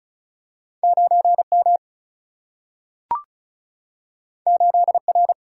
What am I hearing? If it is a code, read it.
Morse code, heard as a keyed tone, about 36 wpm, 700 Hz. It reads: 9 M 8 R
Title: Single-Letter Single-Number 35wpm